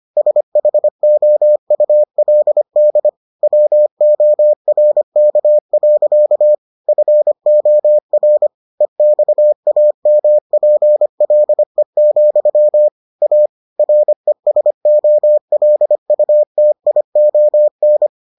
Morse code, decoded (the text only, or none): SHOULD WORK. FOR EXAMPLE, A RESOLUTION